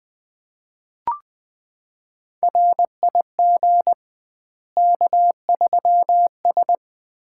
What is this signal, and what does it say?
Morse code, heard as a keyed tone, about 20 wpm, 700 Hz. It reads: RIG K3S
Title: Common QSO Phrases - 2x Word-Spacing - 20wpm